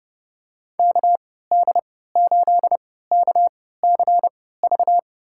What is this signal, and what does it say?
Morse code, heard as a keyed tone, about 31 wpm, spaced as for 17 wpm, 700 Hz. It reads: KD8KCV
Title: USA Call Signs - 6 Characters - 3x Character Spacing - 30wpm